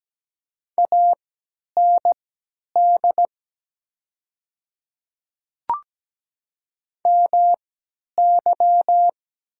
Morse code, heard as a keyed tone, about 17 wpm, spaced as for 10 wpm, 700 Hz. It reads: AND MY
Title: 3x Character Spacing - Top 100 Words - 17wpm